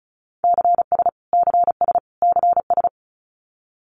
Morse code, heard as a keyed tone, about 34 wpm, 700 Hz. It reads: CS CS CS
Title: Abbreviations Repeated 3x - 35wpm